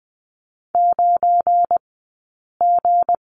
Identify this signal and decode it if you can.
Morse code, heard as a keyed tone, about 20 wpm, spaced as for 8 wpm, 700 Hz. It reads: 9G
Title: 171 Lesson 28d - Number 9 - Two Characters 20wpm